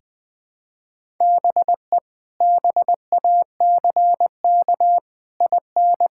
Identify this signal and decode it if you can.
Morse code, heard as a keyed tone, about 20 wpm, 700 Hz. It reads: BE BACK IN
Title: Build A Phrase - 5-Word Phrases - 20wpm